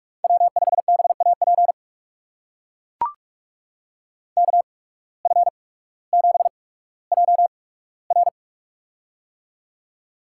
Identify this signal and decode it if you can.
Morse code, heard as a keyed tone, about 44 wpm, spaced as for 19 wpm, 700 Hz. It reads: W5BAP KF7JR
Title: USA Call Signs - 5 Characters - 8x Character Spacing - 45wpm